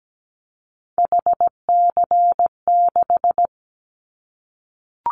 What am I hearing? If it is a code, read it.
Morse code, heard as a keyed tone, about 17 wpm, 700 Hz. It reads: HC6
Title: Callsign Prefixes for Top 100 DXCC - 17wpm